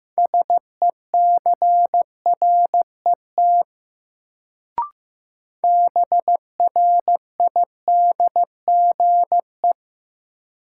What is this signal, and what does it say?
Morse code, heard as a keyed tone, about 16 wpm, 700 Hz. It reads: SECRET BRIDGE
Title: Sentences from Top 1200 Words - New Word Drill Alt - 15wpm